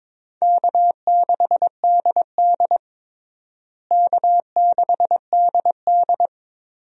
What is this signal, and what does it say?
Morse code, heard as a keyed tone, about 22 wpm, 700 Hz. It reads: K6DD K6DD
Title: USA Call Signs - 4 Characters - Repeated 2x - 22wpm